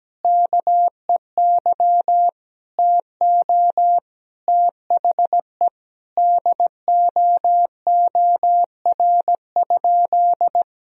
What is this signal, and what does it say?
Morse code, heard as a keyed tone, about 17 wpm, 700 Hz. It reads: KEY TO THE DOOR?
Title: Sentences from Top 1000 Words - Review Part 4 - 17wpm